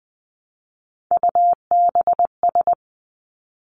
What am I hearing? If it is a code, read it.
Morse code, heard as a keyed tone, about 20 wpm, 700 Hz. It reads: UBS
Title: Three Challenging Letter-Number Combinations 20wpm